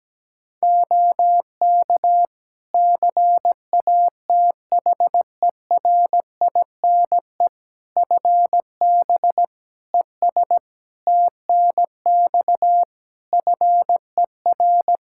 Morse code, heard as a keyed tone, about 17 wpm, 700 Hz. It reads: OK CATHERINE FB ES TNX FER
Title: Practice QSO - Exchange 2 - Info - 17wpm